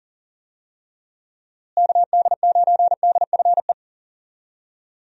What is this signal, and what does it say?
Morse code, heard as a keyed tone, about 40 wpm, 700 Hz. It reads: KD9DFE
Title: USA Call Signs - 6 Characters - Repeated 2x - 40wpm